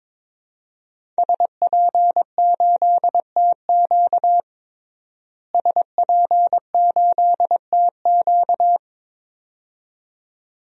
Morse code, heard as a keyed tone, about 21 wpm, 700 Hz. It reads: SP8TQ SP8TQ
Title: DXCC-100 Call Signs - 5 Characters - Repeated 2x - 22wpm